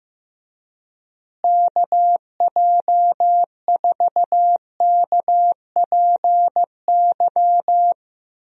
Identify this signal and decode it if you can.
Morse code, heard as a keyed tone, about 15 wpm, 700 Hz. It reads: KJ4KPY
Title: USA Call Signs - 6 Characters - Repeated 2x - 15wpm